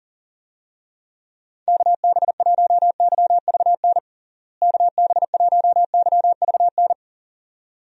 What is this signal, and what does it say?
Morse code, heard as a keyed tone, about 40 wpm, 700 Hz. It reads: KB1YVN KB1YVN
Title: USA Call Signs - 6 Characters - Repeated 2x - 40wpm